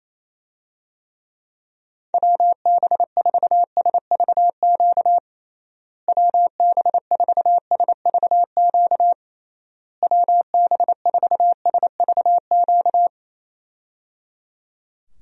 Morse code, heard as a keyed tone, about 27 wpm, 700 Hz. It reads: WB4SVQ WB4SVQ WB4SVQ
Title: USA Call Signs - 6 Characters - Repeated 3x - 28wpm